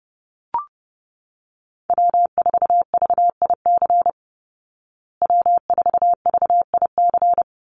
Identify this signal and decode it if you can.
Morse code, heard as a keyed tone, about 30 wpm, 700 Hz. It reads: W4VIC W4VIC
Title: Call Signs Repeated 2x - 30wpm